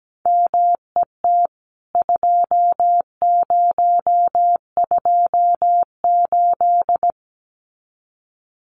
Morse code, heard as a keyed tone, about 17 wpm, 700 Hz. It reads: MET 2028
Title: CWT Exchange 17wpm